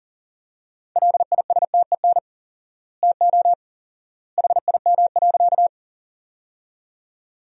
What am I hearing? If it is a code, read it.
Morse code, heard as a keyed tone, about 37 wpm, 700 Hz. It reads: LISTEN TO HIM.
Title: 4x Word-Spacing - Sentences from Top 400 Words - 40wpm